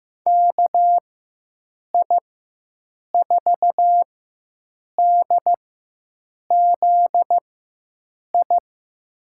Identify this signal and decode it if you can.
Morse code, heard as a keyed tone, about 15 wpm, spaced as for 7 wpm, 700 Hz. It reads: KI4DZI
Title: USA Call Signs - 6 Characters - 4x Character Spacing - 15wpm